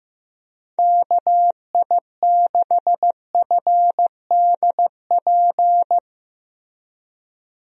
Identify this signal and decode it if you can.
Morse code, heard as a keyed tone, about 15 wpm, 700 Hz. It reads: KI6FDP
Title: USA Call Signs - 6 Characters - Repeated 3x - 15wpm